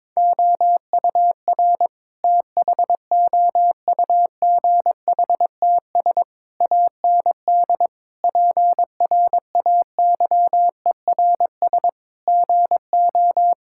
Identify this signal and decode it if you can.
Morse code, heard as a keyed tone, about 22 wpm, 700 Hz. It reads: OUR THOUGHTS AND PRAYERS GO
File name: practice / 22wpm / Sets of 8 Words - 22wpm